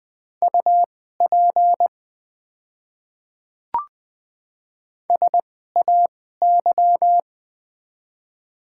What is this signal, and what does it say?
Morse code, heard as a keyed tone, about 20 wpm, spaced as for 15 wpm, 700 Hz. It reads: UP SAY
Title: 2x Character Spacing - Top 300 Words - 20wpm